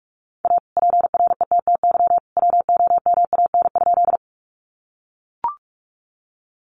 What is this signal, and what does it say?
Morse code, heard as a keyed tone, about 44 wpm, 700 Hz. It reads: A PRETTY WOMAN?
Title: Sentences from Top 1100 Words 45wpm